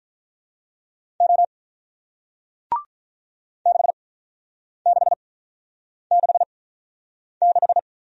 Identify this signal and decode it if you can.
Morse code, heard as a keyed tone, about 48 wpm, spaced as for 9 wpm, 700 Hz. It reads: O 6666